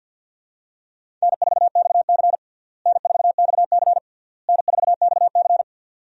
Morse code, heard as a keyed tone, about 52 wpm, 700 Hz. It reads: N4XC N4XC N4XC
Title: USA Call Signs - 4 Characters - Repeated 3x - 50wpm